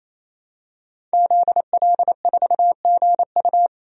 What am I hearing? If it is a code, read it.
Morse code, heard as a keyed tone, about 28 wpm, 700 Hz. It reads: ZL4GU
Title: DXCC-100 Call Signs - 5 Characters - 28wpm